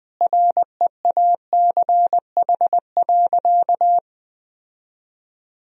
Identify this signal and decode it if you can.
Morse code, heard as a keyed tone, about 20 wpm, 700 Hz. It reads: REACH.
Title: Sentences from Top 1000 Words - Review Part 2 - 20wpm